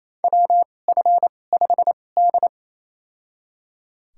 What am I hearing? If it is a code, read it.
Morse code, heard as a keyed tone, about 28 wpm, spaced as for 20 wpm, 700 Hz. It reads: WF5D
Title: USA Call Signs - 4 Characters - 2x Character Spacing - 28wpm